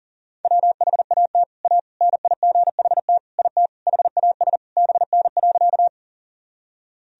Morse code, heard as a keyed tone, about 40 wpm, 700 Hz. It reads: WHAT A NIGHT IT HAS BN.
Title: Sentences from Top 300 Words and Abbreviations 40wpm